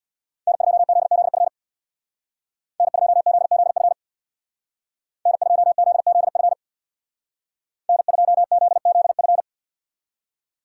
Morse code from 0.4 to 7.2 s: N2ZZF N2ZZF N2ZZF

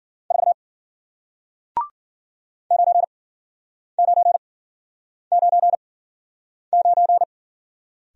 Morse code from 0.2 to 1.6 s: 4

